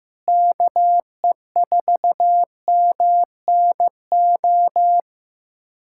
Morse code, heard as a keyed tone, about 15 wpm, 700 Hz. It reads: KE4MNO
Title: USA Callsign DE Callsign 15wpm